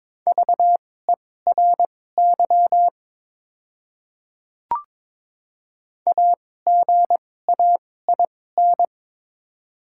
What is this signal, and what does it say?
Morse code, heard as a keyed tone, about 22 wpm, spaced as for 16 wpm, 700 Hz. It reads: VERY AGAIN
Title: 2x Character Spacing - Top 300 Words - 22wpm